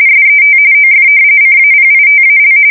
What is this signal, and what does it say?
RTTY, about 45.45 baud. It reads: SET OF CAPITAL
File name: fsk170 / usnavy4